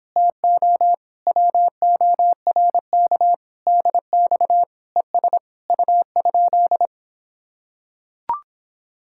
Morse code, heard as a keyed tone, about 26 wpm, 700 Hz. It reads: TO WORK DX ES U?